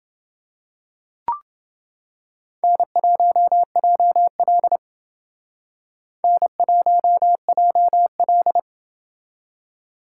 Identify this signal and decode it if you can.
Morse code, heard as a keyed tone, about 30 wpm, 700 Hz. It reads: N1JL N1JL